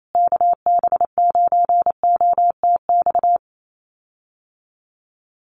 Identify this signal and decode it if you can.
Morse code, heard as a keyed tone, about 28 wpm, 700 Hz. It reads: KB9OTX
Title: USA Call Signs Repeated 2x 28wpm